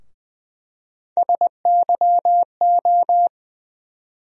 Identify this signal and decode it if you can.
Morse code, heard as a keyed tone, about 20 wpm, 700 Hz. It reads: SYO
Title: USA Call Sign Postfixes - 3 Letters 20wpm